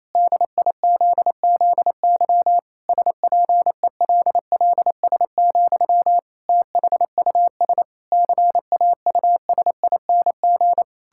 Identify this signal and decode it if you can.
Morse code, heard as a keyed tone, about 28 wpm, 700 Hz. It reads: DIZZY SPELLS, THUS CAUSING
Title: Pangram Challenge v2 - 28wpm